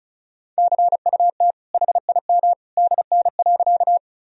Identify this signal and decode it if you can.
Morse code, heard as a keyed tone, about 36 wpm, 700 Hz. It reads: CUT HIM DN.
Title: Sentences from Top 300 Words and Abbreviations 35wpm